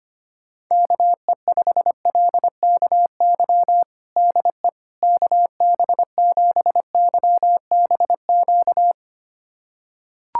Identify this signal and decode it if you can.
Morse code, heard as a keyed tone, about 25 wpm, 700 Hz. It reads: KE5LKY DE KB7YBQ
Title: USA Callsign DE Callsign - 6 Characters - 25wpm